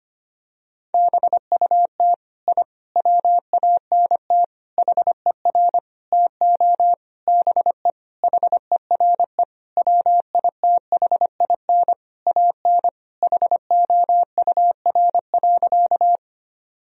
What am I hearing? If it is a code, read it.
Morse code, heard as a keyed tone, about 25 wpm, 700 Hz. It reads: BUT I WANT HER TO BE HERE WITHIN AN HOUR.
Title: Sentences from Top 1000 Words - Review Part 4 - 25wpm